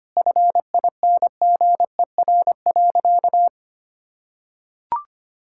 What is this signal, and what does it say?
Morse code, heard as a keyed tone, about 25 wpm, 700 Hz. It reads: FINGER.
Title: Sentences from Top 1600 Words 25wpm